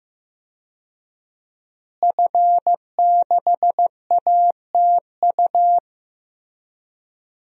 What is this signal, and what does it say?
Morse code, heard as a keyed tone, about 15 wpm, 700 Hz. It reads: F6ATU
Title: DXCC-100 Call Signs - 5 Characters - Repeated 3x - 15wpm